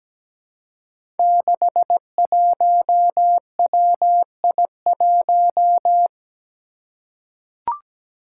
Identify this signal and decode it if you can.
Morse code, heard as a keyed tone, about 17 wpm, 700 Hz. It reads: 61WI1
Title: Five Challenging Letter-Number Combinations 17wpm